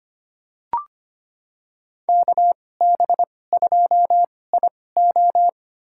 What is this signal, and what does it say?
Morse code, heard as a keyed tone, about 25 wpm, spaced as for 18 wpm, 700 Hz. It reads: KB2IO
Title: USA Call Signs - 5 Characters - 2x Character Spacing - 25wpm